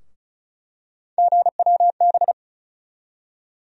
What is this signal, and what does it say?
Morse code, heard as a keyed tone, about 36 wpm, 700 Hz. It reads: GWB
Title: USA Call Sign Postfixes - 3 Letters 35wpm